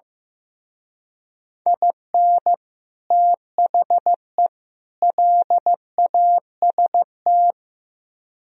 Morse code, heard as a keyed tone, about 17 wpm, 700 Hz. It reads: IN THE LAST
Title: Speed-Racing - Sets of 3 Words 15wpm